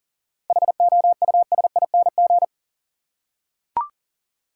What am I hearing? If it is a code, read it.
Morse code, heard as a keyed tone, about 40 wpm, 700 Hz. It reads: HOUSING
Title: Seven Letter Words - 40wpm